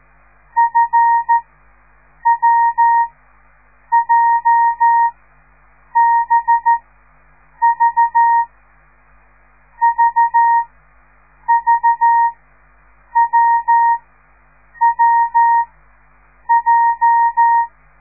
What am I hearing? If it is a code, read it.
Morse code, heard as a keyed tone, about 13 wpm, spaced as for 7 wpm, 930 Hz. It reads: FWJBV VVWWJ